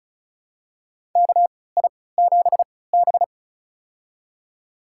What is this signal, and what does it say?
Morse code, heard as a keyed tone, about 36 wpm, spaced as for 20 wpm, 700 Hz. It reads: KI7B
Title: USA Call Signs - 4 Characters - 3x Character Spacing - 35wpm